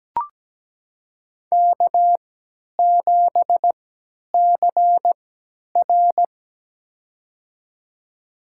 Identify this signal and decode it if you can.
Morse code, heard as a keyed tone, about 17 wpm, spaced as for 10 wpm, 700 Hz. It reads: K7CR
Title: USA Call Signs - 4 Characters - 3x Character Spacing - 17wpm